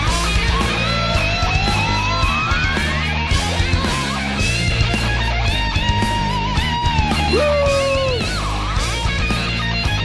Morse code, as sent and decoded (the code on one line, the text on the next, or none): none